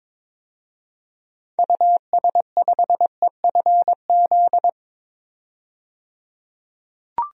Morse code, heard as a keyed tone, about 22 wpm, 700 Hz. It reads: US5EFZ
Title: DXCC-100 Call Signs - 6 Characters - 22wpm